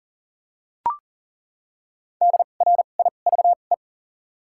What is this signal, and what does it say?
Morse code, heard as a keyed tone, about 39 wpm, 700 Hz. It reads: D R I V E
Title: 2x Character Spacing - Top 500 Words - 40wpm